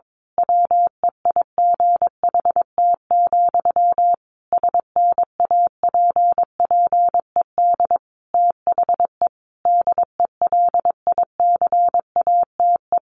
Morse code, heard as a keyed tone, about 22 wpm, 700 Hz. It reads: WEIGHT, SNAPPED THE DELICATE
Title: A Sound Of Thunder 22wpm